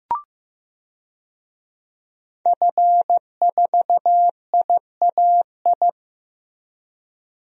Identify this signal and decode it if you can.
Morse code, heard as a keyed tone, about 15 wpm, 700 Hz. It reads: F4IAI
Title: DXCC-100 Call Signs - 5 Characters - Repeated 3x - 15wpm